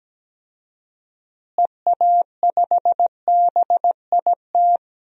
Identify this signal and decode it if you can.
Morse code, heard as a keyed tone, about 17 wpm, 700 Hz. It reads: EA5BIT
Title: DXCC-100 Call Signs - 6 Characters - Repeated 2x - 17wpm